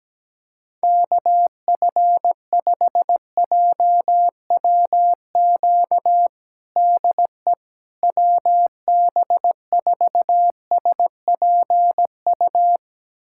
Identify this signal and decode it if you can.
Morse code, heard as a keyed tone, about 17 wpm, 700 Hz. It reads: KF5JWQ DE WB4SPU
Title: USA Callsign DE Callsign - 6 Characters - 17wpm